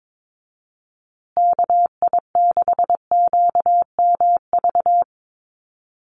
Morse code, heard as a keyed tone, about 22 wpm, 700 Hz. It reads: KI6QMV
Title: USA Call Signs Repeated 3x 22wpm